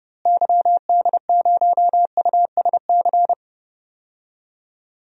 Morse code, heard as a keyed tone, about 30 wpm, 700 Hz. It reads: YD0USC
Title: DXCC-100 Call Signs - 6 Characters - Repeated 2x - 30wpm